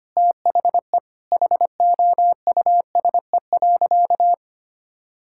Morse code, heard as a keyed tone, about 25 wpm, 700 Hz. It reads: THE HOUSE.
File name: practice / 25wpm / Sentences with 4-Word Phrases - 25wpm